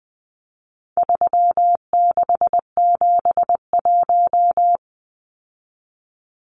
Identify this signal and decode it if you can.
Morse code, heard as a keyed tone, about 20 wpm, 700 Hz. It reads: 3671
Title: Numbers - 4-Digit 20wpm